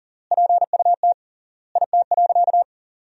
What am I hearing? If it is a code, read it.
Morse code, heard as a keyed tone, about 40 wpm, 700 Hz. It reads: PUT IT.
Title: 3x Word-Spacing - Sentences from Top 200 Words - 40wpm